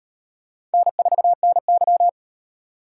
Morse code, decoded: N4NY

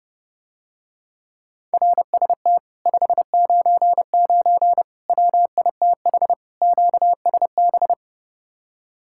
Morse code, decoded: RST 599 WITH QSB

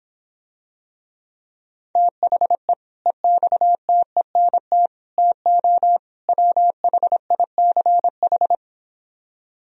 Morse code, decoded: THE EXTENT TO WHICH

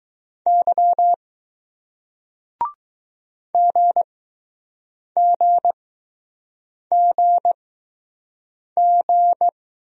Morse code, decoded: Y GGGG

